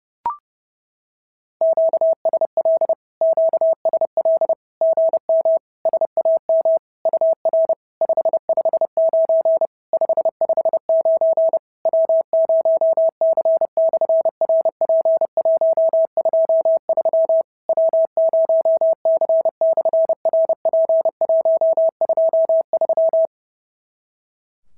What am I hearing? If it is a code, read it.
Morse code, heard as a keyed tone, about 29 wpm, 650 Hz. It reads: QSL QSL GM SAM UR 559 559 W0C/RP123 W0C/RP123
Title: SOTA QSO Practice 30wpm